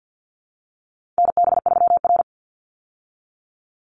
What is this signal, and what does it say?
Morse code, heard as a keyed tone, about 48 wpm, 700 Hz. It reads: NB3R
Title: Call Signs Repeated 3x - 50wpm